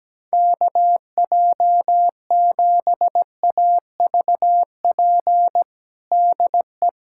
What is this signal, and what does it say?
Morse code, heard as a keyed tone, about 17 wpm, 700 Hz. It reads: KJ7AVP DE
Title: USA Callsign DE Callsign 17wpm